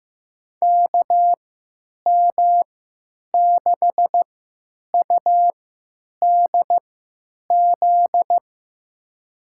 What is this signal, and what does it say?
Morse code, heard as a keyed tone, about 15 wpm, spaced as for 9 wpm, 700 Hz. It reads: KM6UDZ
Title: USA Call Signs - 6 Characters - 3x Character Spacing - 15wpm